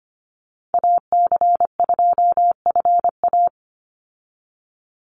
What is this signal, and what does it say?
Morse code, heard as a keyed tone, about 25 wpm, 700 Hz. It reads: AC2FA
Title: Call Signs - 25wpm